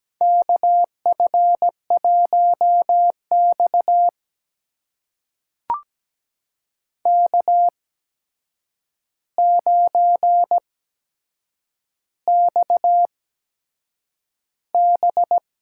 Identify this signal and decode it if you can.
Morse code, heard as a keyed tone, about 17 wpm, spaced as for 7 wpm, 700 Hz. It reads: KF1X K9XB
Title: USA Call Signs - 4 Characters - 8x Character Spacing - 17wpm